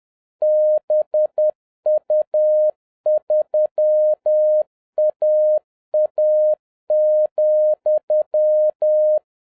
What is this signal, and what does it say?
Morse code, heard as a keyed tone, about 10 wpm, 610 Hz. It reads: BU3AA,